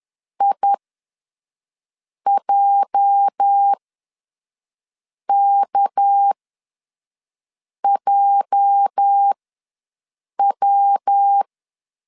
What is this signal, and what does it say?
Morse code, heard as a keyed tone, about 10 wpm, spaced as for 5 wpm, 800 Hz. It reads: IJKJW
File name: LEZIONI / 8 YJ45 / 8 YJ ws 10wpm 4043